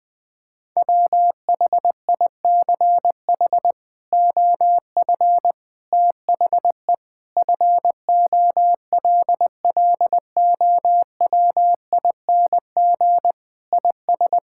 Morse code, heard as a keyed tone, about 21 wpm, 700 Hz. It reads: WHICH OF THE FOLLOWING IS
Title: Sentences with 8-Word Phrases - 20wpm